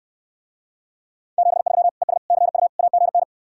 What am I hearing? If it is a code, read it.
Morse code, heard as a keyed tone, about 69 wpm, 700 Hz. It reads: 73 ES CU AGN